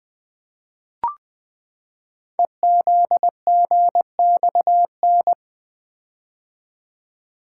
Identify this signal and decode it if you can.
Morse code, heard as a keyed tone, about 20 wpm, 700 Hz. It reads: EZGXN
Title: Code Groups - 5 Letter:Numbers - 20wpm